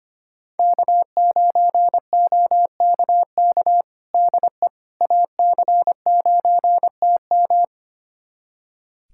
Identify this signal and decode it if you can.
Morse code, heard as a keyed tone, about 25 wpm, 700 Hz. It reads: K9OKK DE AC9TM
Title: USA Callsign DE Callsign - 5 Characters - 25wpm